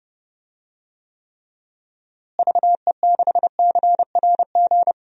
Morse code, heard as a keyed tone, about 32 wpm, 700 Hz. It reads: VE6CRG
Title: DXCC-100 Call Signs - 6 Characters - Repeated 3x - 30wpm